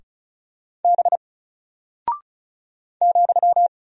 Morse code, heard as a keyed tone, about 36 wpm, 700 Hz. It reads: B ,
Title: Single Character (Letter:Number:Punctuation:Prosign) 35wpm